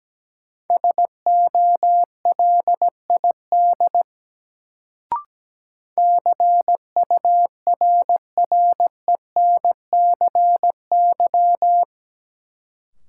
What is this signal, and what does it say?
Morse code, heard as a keyed tone, about 17 wpm, 700 Hz. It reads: SOLID CURRENCY
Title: Sentences from Top 1500 Words - New Word Drill Alt - 17wpm